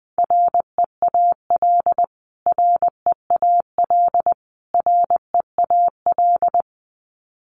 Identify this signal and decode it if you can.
Morse code, heard as a keyed tone, about 20 wpm, 700 Hz. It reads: REAL REAL REAL
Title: Top 500 Words Repeated 20wpm